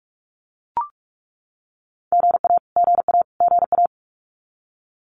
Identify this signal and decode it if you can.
Morse code, heard as a keyed tone, about 44 wpm, 700 Hz. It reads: GA GA GA
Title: Abbreviations Repeated 3x - 45wpm